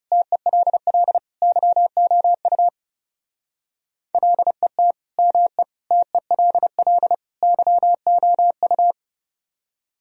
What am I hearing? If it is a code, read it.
Morse code, heard as a keyed tone, about 35 wpm, 700 Hz. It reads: TELL YOU LET ME TELL YOU
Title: Speed-Racing - Sets of 4 Words 30wpm